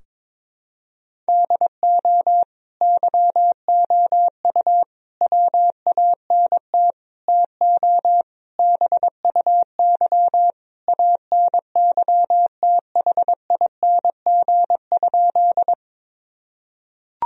Morse code, heard as a keyed tone, about 21 wpm, 700 Hz. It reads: DO YOU WANT TO BUY ANYTHING?
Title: Sentences from Top 1000 Words - Review Part 4 - 22wpm